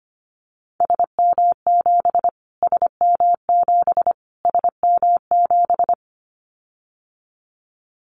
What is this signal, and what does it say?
Morse code, heard as a keyed tone, about 25 wpm, 700 Hz. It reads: SM7 SM7 SM7
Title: Callsign Prefixes for Top 25 DXCC - Repeated 3x - 25wpm